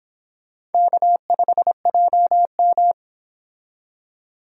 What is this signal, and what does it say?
Morse code, heard as a keyed tone, about 28 wpm, 700 Hz. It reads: K5JM